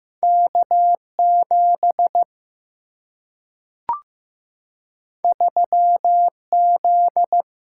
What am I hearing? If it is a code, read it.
Morse code, heard as a keyed tone, about 15 wpm, 700 Hz. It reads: K7 3Z
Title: Single-Letter Single-Number 15wpm